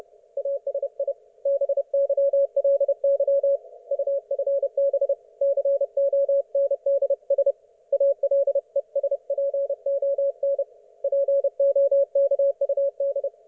Morse code, heard as a keyed tone, about 31 wpm, 560 Hz. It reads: ASI BYLY UFB CONDS ALESPON POKUD
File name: Archiv / CUC179 / ok1fhi cul179